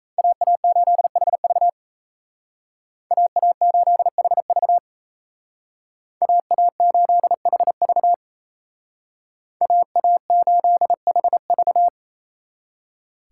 Morse code, heard as a keyed tone, about 36 wpm, 700 Hz. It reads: AA8HV AA8HV AA8HV AA8HV